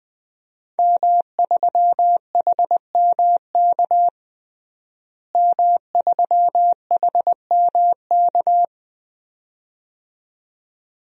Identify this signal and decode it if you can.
Morse code, heard as a keyed tone, about 20 wpm, 700 Hz. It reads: M3HMK M3HMK
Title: DXCC-100 Call Signs - 5 Characters - Repeated 2x - 20wpm